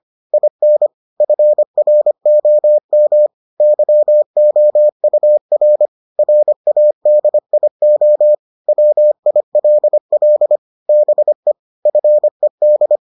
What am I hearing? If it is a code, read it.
Morse code, heard as a keyed tone, about 25 wpm, 600 Hz. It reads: IN FROM YOUR RADIO WILL BE FED